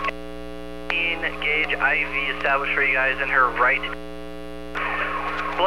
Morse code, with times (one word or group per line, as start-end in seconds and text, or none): none